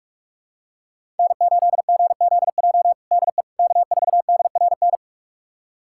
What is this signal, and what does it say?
Morse code, heard as a keyed tone, about 46 wpm, 700 Hz. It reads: N8GZJ DE K4DRN
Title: USA Callsign DE Callsign 45wpm